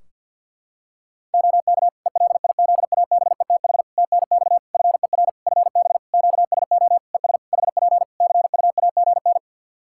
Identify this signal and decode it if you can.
Morse code, heard as a keyed tone, about 47 wpm, 700 Hz. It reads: OK ELIZABETH TNX FER FB QSO ES HP CUAGN
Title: Practice QSO - Exchange 4 - Ending - 50wpm